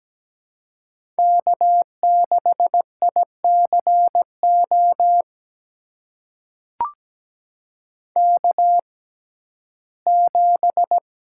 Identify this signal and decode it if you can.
Morse code, heard as a keyed tone, about 17 wpm, spaced as for 12 wpm, 700 Hz. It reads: K6ICO K7
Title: USA Call Signs - 5 Characters - 6x Character Spacing - 17wpm